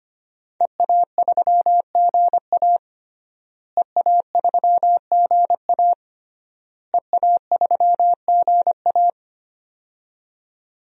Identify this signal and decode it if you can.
Morse code, heard as a keyed tone, about 25 wpm, 700 Hz. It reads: EA3GA EA3GA EA3GA
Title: DXCC-100 Call Signs - 5 Characters - Repeated 3x - 25wpm